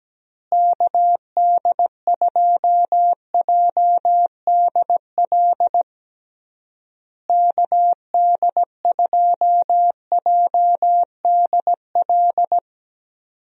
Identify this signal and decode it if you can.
Morse code, heard as a keyed tone, about 17 wpm, 700 Hz. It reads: KD2JDL KD2JDL
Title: USA Call Signs - 6 Characters - Repeated 3x - 17wpm